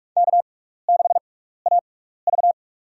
Morse code, heard as a keyed tone, about 44 wpm, spaced as for 15 wpm, 700 Hz. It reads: K6AV
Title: USA Call Signs - 4 Characters - 6x Character Spacing - 45wpm